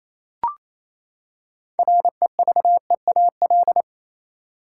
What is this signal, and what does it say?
Morse code, heard as a keyed tone, about 28 wpm, 700 Hz. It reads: REVEAL
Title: Six Letter Words - 28wpm